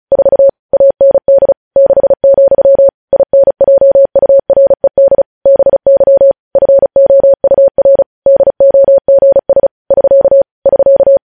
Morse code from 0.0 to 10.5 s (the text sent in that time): 4 AND 6, INJURED BY FOUR DOGS <SK>